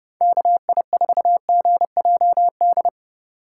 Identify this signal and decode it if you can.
Morse code, heard as a keyed tone, about 32 wpm, 700 Hz. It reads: KI4GJD
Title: USA Call Signs - 6 Characters - 2x Character Spacing - 30wpm